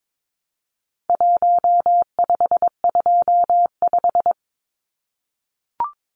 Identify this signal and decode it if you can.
Morse code, heard as a keyed tone, about 22 wpm, 700 Hz. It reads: 1525
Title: Numbers - Contest Style - 4-Digit 22wpm